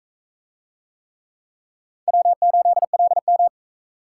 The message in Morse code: W8LM

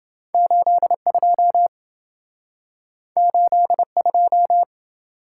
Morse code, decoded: 82 82

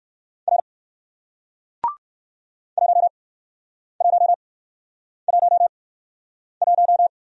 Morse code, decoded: R 1111

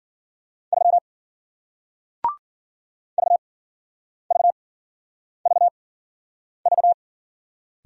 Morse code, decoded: 3 VVVV

